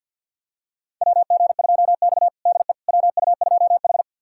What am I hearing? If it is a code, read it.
Morse code, heard as a keyed tone, about 52 wpm, 700 Hz. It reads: WG2X DE WU2H
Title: USA Callsign DE Callsign - 4 Characters - 50wpm